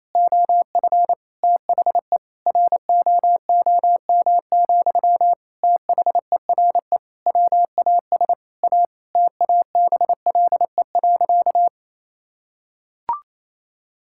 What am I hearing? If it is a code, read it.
Morse code, heard as a keyed tone, about 28 wpm, 700 Hz. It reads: OF THE ROOM, THERE WAS A TABLE.
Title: Sentences from Top 1000 Words - Review Part 6 - 28wpm